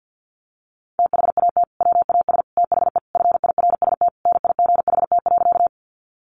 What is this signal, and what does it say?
Morse code, heard as a keyed tone, about 50 wpm, 700 Hz. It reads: THAT WAS THE FIRST NIGHT.
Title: Sentences from Top 300 Words 50wpm